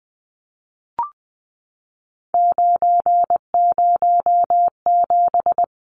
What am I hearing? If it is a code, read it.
Morse code, heard as a keyed tone, about 20 wpm, 700 Hz. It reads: 907
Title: Numbers - 3-Digit 20wpm